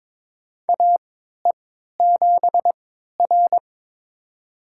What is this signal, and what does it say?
Morse code, heard as a keyed tone, about 22 wpm, spaced as for 13 wpm, 700 Hz. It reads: AE7R
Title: USA Call Signs - 4 Characters - 3x Character Spacing - 22wpm